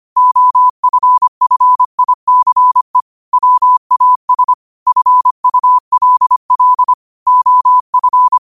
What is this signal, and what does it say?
Morse code, heard as a keyed tone, about 25 wpm, 1000 Hz. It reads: OFFICE WAS FULL OF